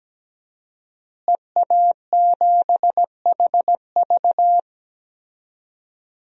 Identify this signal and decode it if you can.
Morse code, heard as a keyed tone, about 17 wpm, 700 Hz. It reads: EA7HV
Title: DXCC-100 Call Signs - 5 Characters - Repeated 2x - 17wpm